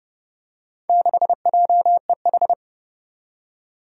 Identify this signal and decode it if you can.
Morse code, heard as a keyed tone, about 31 wpm, 700 Hz. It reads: 6JEH
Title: Four Challenging Letter-Number Combinations 30wpm